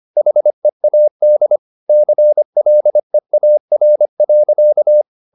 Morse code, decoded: HEAD CLEAR.

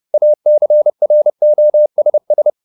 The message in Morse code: ACROSS

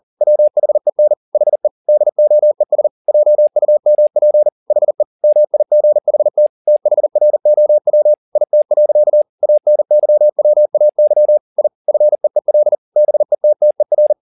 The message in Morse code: WHEN HE DOES JUMP HE MIGHT THROW IT. ANYWAY I FEEL BETTER

